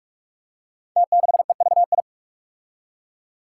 Morse code, heard as a keyed tone, about 44 wpm, 700 Hz. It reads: T6EVI
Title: Five Challenging Letter-Number Combinations 45wpm